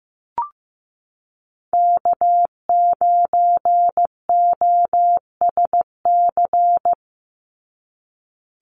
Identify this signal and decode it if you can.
Morse code, heard as a keyed tone, about 15 wpm, 700 Hz. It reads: K9OSC
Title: NA SOTA Chaser Callsigns 15wpm